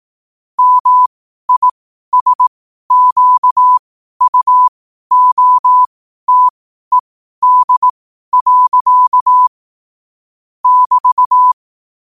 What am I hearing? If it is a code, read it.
Morse code, heard as a keyed tone, about 18 wpm, spaced as for 12 wpm, 1000 Hz. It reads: MISQUOTED. <BT>
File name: qotdfiles / 12wpm